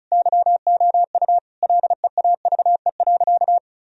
Morse code, heard as a keyed tone, about 34 wpm, 700 Hz. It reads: YOU LEAVE.
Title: Sentences from Top 1000 Words - Review Part 4 - 35wpm